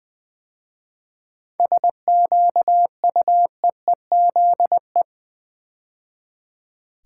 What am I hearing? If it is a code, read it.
Morse code, heard as a keyed tone, about 20 wpm, 700 Hz. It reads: SQUEEZE
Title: 7 Letter Words - First 3 Letters Given - 20wpm